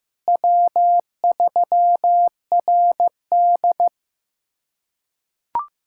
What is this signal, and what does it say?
Morse code, heard as a keyed tone, about 15 wpm, 700 Hz. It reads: W3RD
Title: USA Callsign DE Callsign - 4 Characters - 15wpm